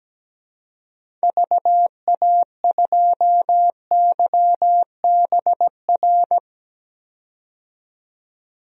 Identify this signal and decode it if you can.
Morse code, heard as a keyed tone, about 17 wpm, 700 Hz. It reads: VA2YBR
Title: DXCC-100 Call Signs - 6 Characters - 17wpm